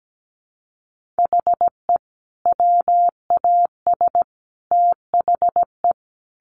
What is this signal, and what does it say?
Morse code, heard as a keyed tone, about 17 wpm, 700 Hz. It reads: HE WAS THE
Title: Sentences from Top 1500 Words 17wpm